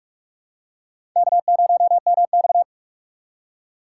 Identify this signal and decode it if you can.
Morse code, heard as a keyed tone, about 47 wpm, 700 Hz. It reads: K0MX
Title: USA Call Signs - 4 Characters - 6x Character Spacing - 45wpm